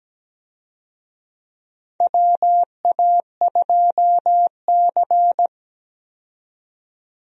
Morse code, heard as a keyed tone, about 17 wpm, 700 Hz. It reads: WA2C
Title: USA Call Signs - 4 Characters - 17wpm